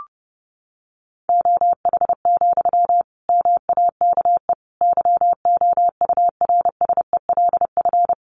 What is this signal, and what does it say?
Morse code, heard as a keyed tone, about 30 wpm, 700 Hz. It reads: OH, MAKE YOURSELF
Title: Sentences from Top 700 Words 30wpm